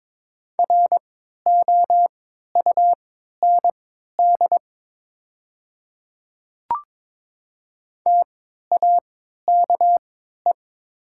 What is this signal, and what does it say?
Morse code, heard as a keyed tone, about 22 wpm, spaced as for 13 wpm, 700 Hz. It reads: ROUND TAKE
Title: 3x Character Spacing - Top 200 Words - 22wpm